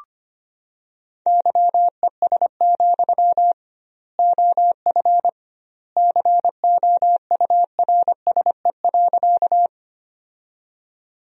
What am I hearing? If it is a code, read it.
Morse code, heard as a keyed tone, about 25 wpm, 700 Hz. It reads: YES, OF COURSE.
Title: 2x Word-Spacing - Sentences from Top 500 Words - 25wpm